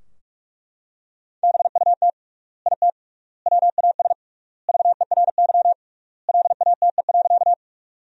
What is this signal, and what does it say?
Morse code, heard as a keyed tone, about 42 wpm, 700 Hz. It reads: BUT IT WAS VERY LATE.
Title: 3x Word-Spacing - Sentences from Top 300 Words - 45wpm